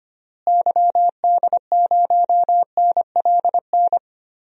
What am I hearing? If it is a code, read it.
Morse code, heard as a keyed tone, about 25 wpm, 700 Hz. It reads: YD0NLN
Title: DXCC-100 Call Signs - 6 Characters - Repeated 3x - 25wpm